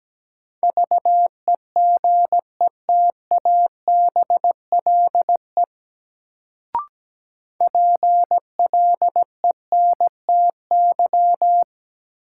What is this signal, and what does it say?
Morse code, heard as a keyed tone, about 17 wpm, 700 Hz. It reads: VEGETABLE PLENTY
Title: Sentences from Top 1300 Words - New Word Drill - 17wpm